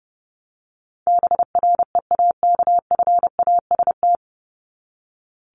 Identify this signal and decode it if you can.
Morse code, heard as a keyed tone, about 30 wpm, 700 Hz. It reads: BREAKFAST
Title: Top 1251-1500 Words 30wpm